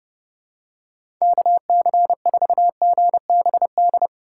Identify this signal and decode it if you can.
Morse code, heard as a keyed tone, about 30 wpm, 700 Hz. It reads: KC4GBD
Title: USA Call Signs - 6 Characters - 3x Character Spacing - 30wpm